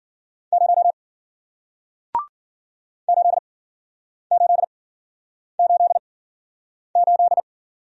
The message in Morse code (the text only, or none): 0 8888